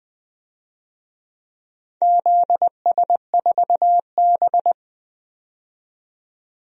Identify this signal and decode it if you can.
Morse code, heard as a keyed tone, about 20 wpm, 700 Hz. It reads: ZS4B
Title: DXCC-100 Call Signs - 4 Characters - 20wpm